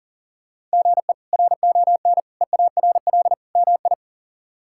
Morse code, heard as a keyed tone, about 40 wpm, 700 Hz. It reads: GE RON EARL MI
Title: K1USN SST QSO Practice - 40wpm